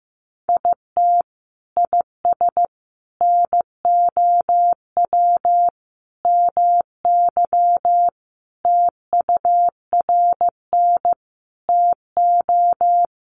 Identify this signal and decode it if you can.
Morse code, heard as a keyed tone, about 15 wpm, 700 Hz. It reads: IT IS NOW MY TURN TO
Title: Sentences from Top 200 Words 15wpm